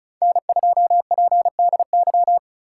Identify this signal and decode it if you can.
Morse code, heard as a keyed tone, about 34 wpm, 700 Hz. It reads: N2PDY
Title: USA Call Signs - 3x Character Spacing - 35wpm